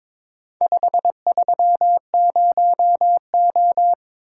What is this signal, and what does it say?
Morse code, heard as a keyed tone, about 22 wpm, 700 Hz. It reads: 530O
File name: practice / 22wpm / Four Letter-Number Combinations - 22wpm